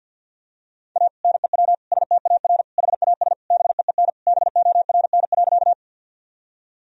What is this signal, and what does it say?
Morse code, heard as a keyed tone, about 52 wpm, 700 Hz. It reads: A NEW STAR HAS BEEN BORN.
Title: Sentences from Top 1000 Words - Review Part 2 - 50wpm